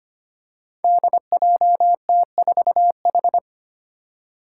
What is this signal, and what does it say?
Morse code, heard as a keyed tone, about 25 wpm, 700 Hz. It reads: DJT4H
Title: Five Challenging Letter-Number Combinations 25wpm